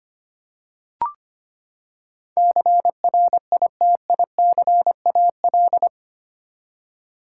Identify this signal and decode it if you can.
Morse code, heard as a keyed tone, about 25 wpm, 700 Hz. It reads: CRITICAL
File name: Eight Letter Words - 25wpm